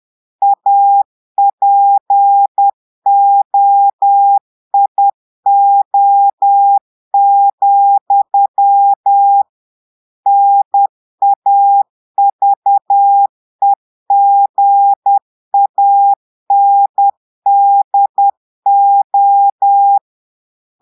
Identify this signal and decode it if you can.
Morse code, heard as a keyed tone, about 10 wpm, 800 Hz. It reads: APOIO, NAVEGANDO
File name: Portugais - texte en clair 1 - vitesse - 10 mots minute